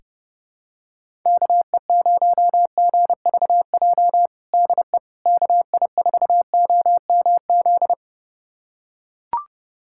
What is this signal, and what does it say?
Morse code, heard as a keyed tone, about 30 wpm, 700 Hz. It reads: KE0GVJ DE KI4OMZ
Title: USA Callsign DE Callsign 30wpm